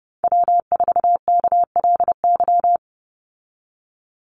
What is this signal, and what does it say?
Morse code, heard as a keyed tone, about 30 wpm, 700 Hz. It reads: W4KLY
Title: Call Signs Repeated 2x - 30wpm